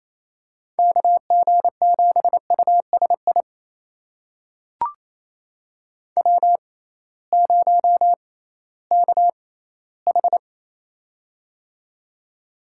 Morse code, heard as a keyed tone, about 28 wpm, spaced as for 16 wpm, 700 Hz. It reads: KG7USI W0KH
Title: USA Call Signs - 6x Character Spacing - 28wpm